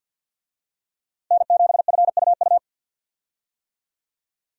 Morse code, heard as a keyed tone, about 47 wpm, 700 Hz. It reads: N7FUU
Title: USA Call Signs - 5 Characters - Repeated 3x - 50wpm